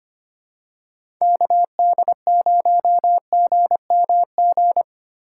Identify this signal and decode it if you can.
Morse code, heard as a keyed tone, about 25 wpm, 700 Hz. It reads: KD0GMG
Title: USA Call Signs - 6 Characters - 6x Character Spacing - 25wpm